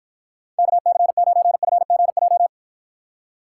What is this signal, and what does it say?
Morse code, heard as a keyed tone, about 53 wpm, 700 Hz. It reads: KC9FGJ